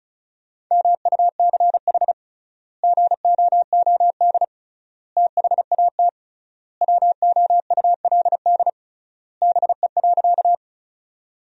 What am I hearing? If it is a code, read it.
Morse code, heard as a keyed tone, about 34 wpm, 700 Hz. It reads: MUCH GOOD THAT WOULD BE.
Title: 3x Word-Spacing - Sentences from Top 200 Words - 35wpm